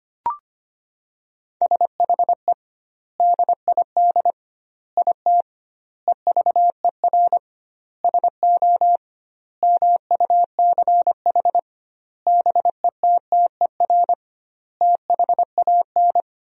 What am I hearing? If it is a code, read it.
Morse code, heard as a keyed tone, about 25 wpm, 700 Hz. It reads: SHE DID IT EVER SO MUCH BETTER THAN
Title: 2x Word-Spacing - Sentences from Top 400 Words - 25wpm